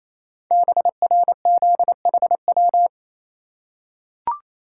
Morse code, heard as a keyed tone, about 27 wpm, 700 Hz. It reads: BRZHW
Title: Code Groups - 5 Letters - 28wpm